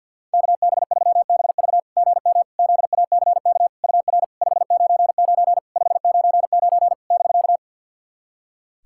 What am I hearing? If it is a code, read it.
Morse code, heard as a keyed tone, about 48 wpm, 690 Hz. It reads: KB3BV GM ZACK UR 599 599 <BK>
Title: SOTA QSO Practice 50wpm